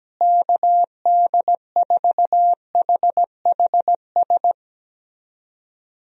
Morse code, heard as a keyed tone, about 17 wpm, 700 Hz. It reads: KD4HHS
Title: USA Call Signs - 6 Characters - Repeated 2x - 17wpm